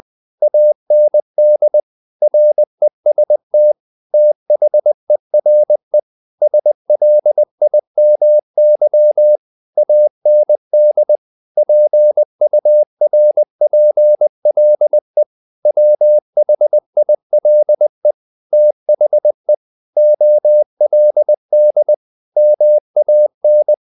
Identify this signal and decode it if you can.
Morse code, heard as a keyed tone, about 20 wpm, 600 Hz. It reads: AND REST THERE SLIMY AND PURPLE WHILE THE OLD MAN
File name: the old man and the sea / the old man and the sea part 1 20wpm